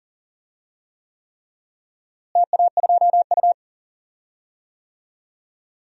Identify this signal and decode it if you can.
Morse code, heard as a keyed tone, about 40 wpm, 700 Hz. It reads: TA2U
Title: DXCC-100 Call Signs - 4 Characters - Repeated 3x - 40wpm